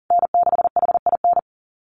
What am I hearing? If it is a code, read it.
Morse code, heard as a keyed tone, about 40 wpm, 700 Hz. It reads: N6HIN
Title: USA Call Signs Repeated 2x 40wpm